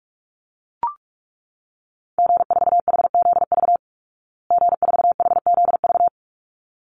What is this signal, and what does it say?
Morse code, heard as a keyed tone, about 44 wpm, 700 Hz. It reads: G4HZV G4HZV
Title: Call Signs Repeated 3x - 45wpm